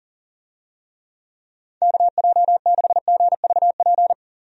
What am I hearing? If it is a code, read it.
Morse code, heard as a keyed tone, about 40 wpm, 700 Hz. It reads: KJ6GVP
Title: USA Call Signs - 6 Characters - Repeated 3x - 40wpm